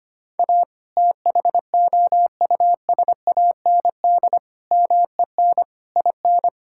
Morse code, heard as a keyed tone, about 25 wpm, 700 Hz. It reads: A THOUSAND MEN IN
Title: Sentences from Top 1000 Words - Review Part 6 - 25wpm